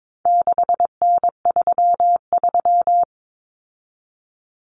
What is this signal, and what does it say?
Morse code, heard as a keyed tone, about 22 wpm, 700 Hz. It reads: 6N33
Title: Common Cut Numbers - 4-Digit 22wpm